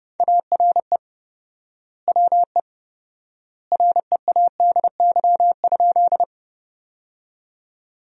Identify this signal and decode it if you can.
Morse code, heard as a keyed tone, about 31 wpm, 700 Hz. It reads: ARE WE READY?
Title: 4x Word-Spacing - Sentences from Top 400 Words - 30wpm